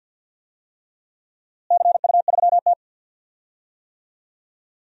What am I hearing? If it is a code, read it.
Morse code, heard as a keyed tone, about 51 wpm, 700 Hz. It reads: CU3T
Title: DXCC-100 Call Signs - 4 Characters - Repeated 3x - 50wpm